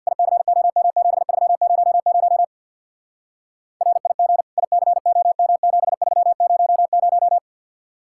Morse code, heard as a keyed tone, about 64 wpm, 700 Hz. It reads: ICOM7300 RIG ICOM7300